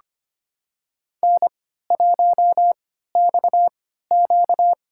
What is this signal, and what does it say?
Morse code, heard as a keyed tone, about 25 wpm, spaced as for 14 wpm, 700 Hz. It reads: N1XQ
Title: USA Call Signs - 4 Characters - 3x Character Spacing - 25wpm